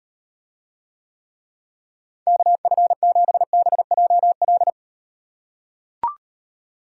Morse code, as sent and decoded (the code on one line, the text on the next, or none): -.- ..-. --... -... .--- .-..
KF7BJL